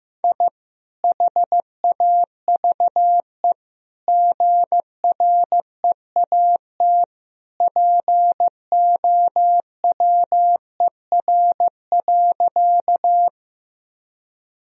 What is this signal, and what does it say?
Morse code, heard as a keyed tone, about 15 wpm, 700 Hz. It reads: I HAVE GREAT POWER.
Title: Sentences from Top 1000 Words - Review Part 2 - 15wpm